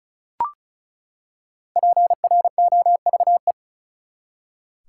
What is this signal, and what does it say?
Morse code, heard as a keyed tone, about 36 wpm, 700 Hz. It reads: PROVE
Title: Five Letter Words - 35wpm